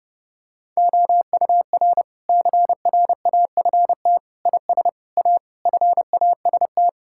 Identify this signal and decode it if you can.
Morse code, heard as a keyed tone, about 29 wpm, 700 Hz. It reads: OUR CRAFT IS A FAST
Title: Sentences from Top 1000 Words - Review Part 6 - 30wpm